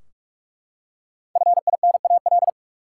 Morse code, heard as a keyed tone, about 43 wpm, 700 Hz. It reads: FINAL
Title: Five Letter Words - 45wpm